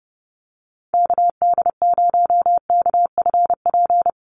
Morse code, heard as a keyed tone, about 30 wpm, 700 Hz. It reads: KD0KFP
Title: USA Call Signs 30wpm